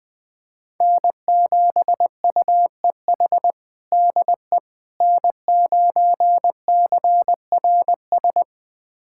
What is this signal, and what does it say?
Morse code, heard as a keyed tone, about 20 wpm, 700 Hz. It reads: N7UEH DE N9CRS
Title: USA Callsign DE Callsign - 5 Characters - 20wpm